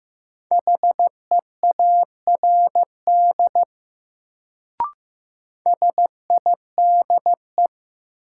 Morse code, heard as a keyed tone, about 15 wpm, 700 Hz. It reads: HEARD SIDE
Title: Sentences from Top 400 Words - New Word Drill - 15wpm